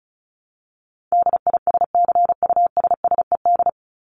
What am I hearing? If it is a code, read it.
Morse code, heard as a keyed tone, about 34 wpm, 700 Hz. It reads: DISCUSSED
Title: ED Suffix - Top 100 Words with Base Words 35wpm